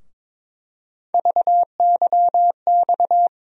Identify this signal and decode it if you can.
Morse code, heard as a keyed tone, about 22 wpm, 700 Hz. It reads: VYX
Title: USA Call Sign Postfixes - 3 Letters 22wpm